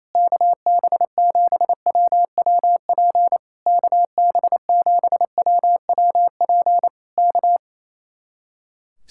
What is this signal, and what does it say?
Morse code, heard as a keyed tone, about 27 wpm, 700 Hz. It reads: KB7WWP KB7WWP K
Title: Speed-Racing Common QSO Phrases Encore - 28wpm